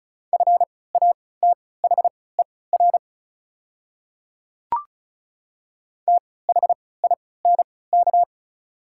Morse code, heard as a keyed tone, about 36 wpm, spaced as for 20 wpm, 700 Hz. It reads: FATHER THINK
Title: 3x Character Spacing - Top 300 Words - 35wpm